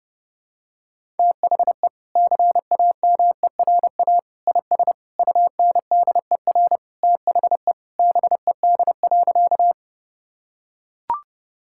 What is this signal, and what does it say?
Morse code, heard as a keyed tone, about 30 wpm, 700 Hz. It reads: THE CAMERA IS UNDER THE BED.
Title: Sentences from Top 1000 Words - Review Part 5 - 30wpm